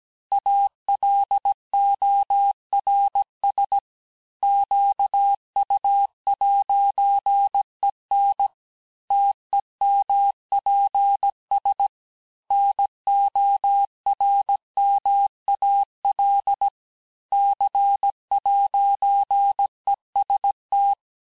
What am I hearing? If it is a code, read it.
Morse code, heard as a keyed tone, about 17 wpm, 800 Hz. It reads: ALORS QU'EN TEMPS NORMAL C'EST
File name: corrige-cours-1581-fin-du-QSO-vitesse-16-mpm